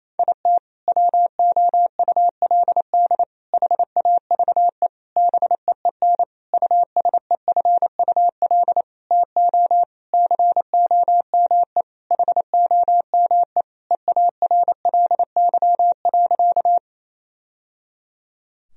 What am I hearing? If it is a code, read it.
Morse code, heard as a keyed tone, about 28 wpm, 700 Hz. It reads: IT WOULD HAVE BEEN USEFUL TO COME HOME EARLY.
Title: Sentences from Top 1000 Words - Review Part 4 - 28wpm